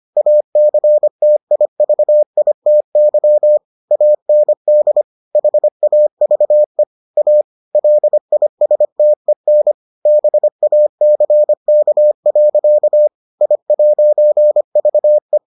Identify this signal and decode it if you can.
Morse code, heard as a keyed tone, about 25 wpm, 600 Hz. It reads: ACTIVITY AND HAVE A LISTEN BACK. I'VE